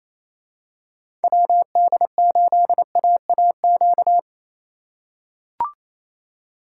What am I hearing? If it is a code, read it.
Morse code, heard as a keyed tone, about 28 wpm, 700 Hz. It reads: WD8AAQ
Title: USA Call Signs - 6 Characters - 3x Character Spacing - 28wpm